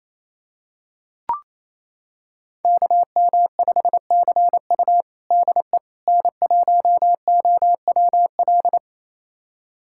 Morse code, heard as a keyed tone, about 27 wpm, 700 Hz. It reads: KM5CU DE N1OWL
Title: USA Callsign DE Callsign - 5 Characters - 28wpm